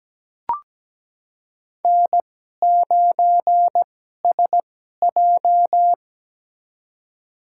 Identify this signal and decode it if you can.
Morse code, heard as a keyed tone, about 17 wpm, spaced as for 12 wpm, 700 Hz. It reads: N9SJ
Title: USA Call Signs - 4 Characters - 2x Character Spacing - 17wpm